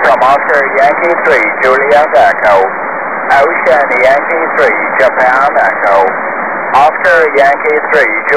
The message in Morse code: none